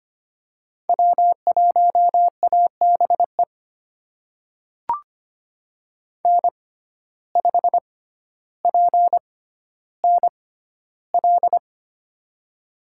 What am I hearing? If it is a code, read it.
Morse code, heard as a keyed tone, about 25 wpm, spaced as for 13 wpm, 700 Hz. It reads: W1ABE N5PNL
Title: USA Call Signs - 5 Characters - 6x Character Spacing - 25wpm